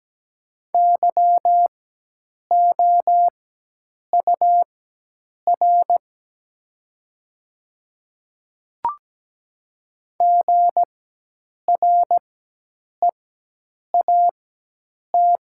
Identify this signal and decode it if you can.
Morse code, heard as a keyed tone, about 17 wpm, spaced as for 8 wpm, 700 Hz. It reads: YOUR GREAT
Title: 4x Character Spacing - Top 300 Words - 17wpm